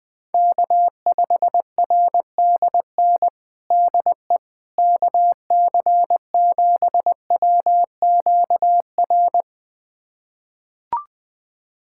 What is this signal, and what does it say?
Morse code, heard as a keyed tone, about 21 wpm, 700 Hz. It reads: K5RDN DE KC7WQR
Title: USA Callsign DE Callsign 20wpm